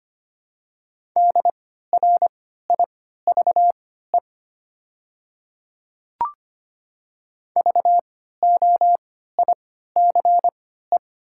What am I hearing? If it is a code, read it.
Morse code, heard as a keyed tone, about 25 wpm, spaced as for 14 wpm, 700 Hz. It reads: DRIVE VOICE
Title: 3x Character Spacing - Top 500 Words - 25wpm